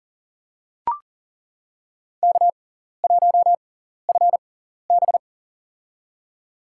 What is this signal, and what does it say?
Morse code, heard as a keyed tone, about 40 wpm, spaced as for 14 wpm, 700 Hz. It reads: K1FB
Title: USA Call Signs - 4 Characters - 6x Character Spacing - 40wpm